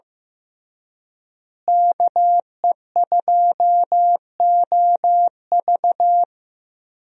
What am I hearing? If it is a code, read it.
Morse code, heard as a keyed tone, about 15 wpm, 700 Hz. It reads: KE2OV
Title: USA Call Signs - 5 Characters - Repeated 3x - 15wpm